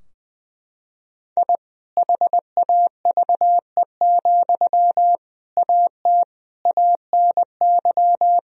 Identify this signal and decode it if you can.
Morse code, heard as a keyed tone, about 20 wpm, 700 Hz. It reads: I HAVE, AT ANY
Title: Sentences from Top 1000 Words - Review Part 4 - 20wpm